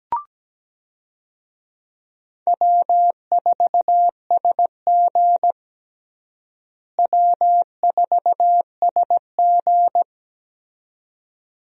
Morse code, heard as a keyed tone, about 17 wpm, 700 Hz. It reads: W4SG W4SG
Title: USA Call Signs - 4 Characters - Repeated 2x - 17wpm